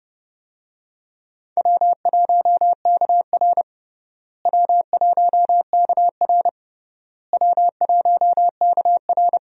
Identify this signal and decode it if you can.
Morse code, heard as a keyed tone, about 30 wpm, 700 Hz. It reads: W1KR W1KR W1KR
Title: USA Call Signs - 4 Characters - Repeated 3x - 30wpm